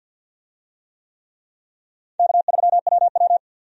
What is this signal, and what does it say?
Morse code, heard as a keyed tone, about 50 wpm, 700 Hz. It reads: K3WW
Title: USA Call Signs - 4 Characters - 50wpm